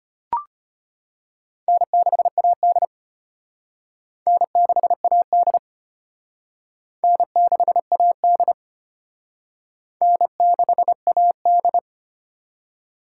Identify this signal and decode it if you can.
Morse code, heard as a keyed tone, about 31 wpm, 700 Hz. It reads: N6AD N6AD N6AD N6AD